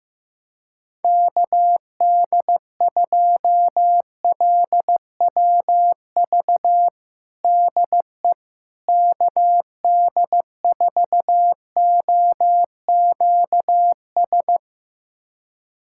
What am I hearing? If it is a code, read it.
Morse code, heard as a keyed tone, about 15 wpm, 700 Hz. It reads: KD2LWV DE KD4OQS
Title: USA Callsign DE Callsign - 6 Characters - 15wpm